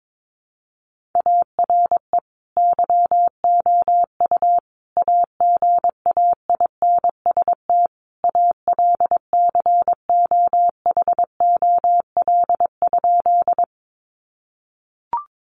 Morse code, heard as a keyed tone, about 22 wpm, 700 Hz. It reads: ARE YOU AGAINST ALCOHOL?
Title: Sentences from Top 1400 Words 22wpm